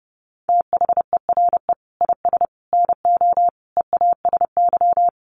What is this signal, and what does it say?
Morse code, heard as a keyed tone, about 30 wpm, 700 Hz. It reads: THERE IS NO EASY
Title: Sentences from Top 1300 Words 30wpm